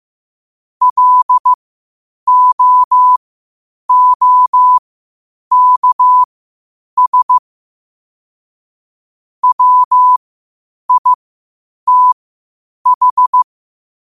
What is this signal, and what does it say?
Morse code, heard as a keyed tone, about 15 wpm, spaced as for 8 wpm, 1000 Hz. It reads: LOOKS WITH
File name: qotdfiles / 8wpm